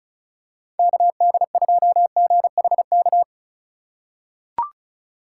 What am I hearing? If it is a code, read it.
Morse code, heard as a keyed tone, about 34 wpm, 700 Hz. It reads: KD2GHK
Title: USA Call Signs - 6 Characters - 3x Character Spacing - 35wpm